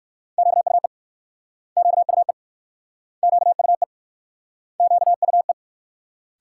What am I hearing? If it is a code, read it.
Morse code, heard as a keyed tone, about 63 wpm, 690 Hz. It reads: QUE QUE QUE QUE